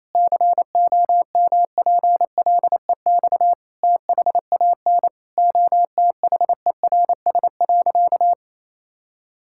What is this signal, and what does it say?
Morse code, heard as a keyed tone, about 28 wpm, 700 Hz. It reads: COMPLEX THAN OTHERS.
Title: Sentences from Top 1000 Words - Review Part 6 - 28wpm